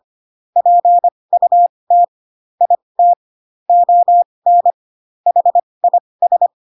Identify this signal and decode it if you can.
Morse code, heard as a keyed tone, about 25 wpm, spaced as for 20 wpm, 700 Hz. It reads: PUT IT ON HIS